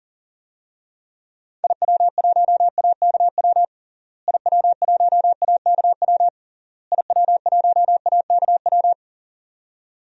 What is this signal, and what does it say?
Morse code, heard as a keyed tone, about 40 wpm, 700 Hz. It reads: IW1AKW IW1AKW IW1AKW
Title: DXCC-100 Call Signs - 6 Characters - Repeated 3x - 40wpm